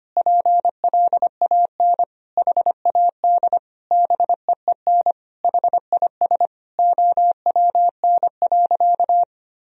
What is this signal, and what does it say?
Morse code, heard as a keyed tone, about 25 wpm, 700 Hz. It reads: PLAN HAD BEEN HIS OWN.
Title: Sentences from Top 1000 Words - Review Part 2 - 25wpm